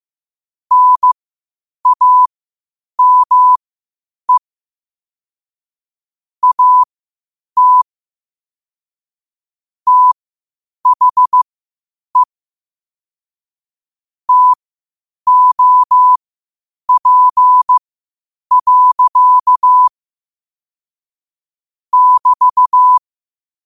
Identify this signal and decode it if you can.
Morse code, heard as a keyed tone, about 15 wpm, spaced as for 8 wpm, 1000 Hz. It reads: NAME AT THE TOP. <BT>
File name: qotdfiles / 8wpm